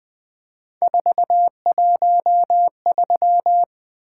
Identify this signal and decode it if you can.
Morse code, heard as a keyed tone, about 20 wpm, 700 Hz. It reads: 413
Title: Three Letter-Number Combinations - 20wpm